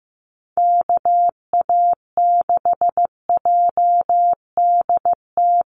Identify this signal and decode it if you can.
Morse code, heard as a keyed tone, about 15 wpm, 700 Hz. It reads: KA6JDT
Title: USA Call Signs 15wpm